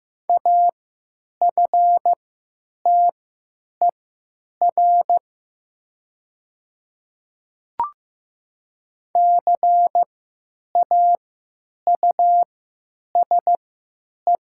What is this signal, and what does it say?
Morse code, heard as a keyed tone, about 15 wpm, spaced as for 9 wpm, 700 Hz. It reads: AFTER CAUSE
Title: 3x Character Spacing - Top 200 Words - 15wpm